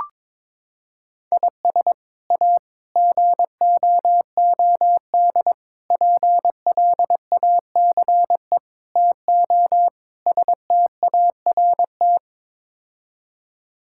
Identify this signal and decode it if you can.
Morse code, heard as a keyed tone, about 22 wpm, 700 Hz. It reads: IS A GOOD PLACE TO START
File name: practice / 22wpm / Sets of 6 Words - 22wpm